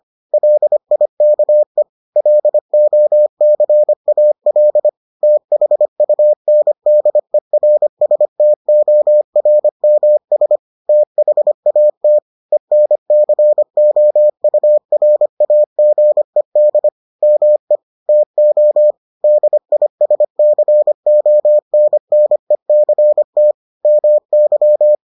Morse code, heard as a keyed tone, about 25 wpm, 600 Hz. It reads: LIKE LOCAL THUNDERSTORMS THAT ENCOURAGED ME TO DISCONNECT MY